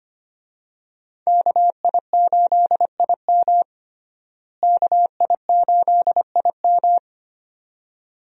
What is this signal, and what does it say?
Morse code, heard as a keyed tone, about 25 wpm, 700 Hz. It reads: KI8IM KI8IM
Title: USA Call Signs - 5 Characters - Repeated 2x - 25wpm